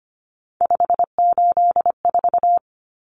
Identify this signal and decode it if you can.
Morse code, heard as a keyed tone, about 25 wpm, 700 Hz. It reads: 584
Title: Numbers - 3-Digit 25wpm